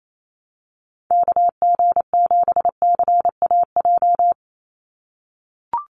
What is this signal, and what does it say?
Morse code, heard as a keyed tone, about 28 wpm, 700 Hz. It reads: KG7CAJ
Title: USA Call Signs 28wpm